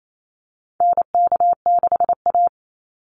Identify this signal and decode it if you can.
Morse code, heard as a keyed tone, about 28 wpm, 700 Hz. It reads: NK6A
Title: NA SOTA Chaser Callsigns 28wpm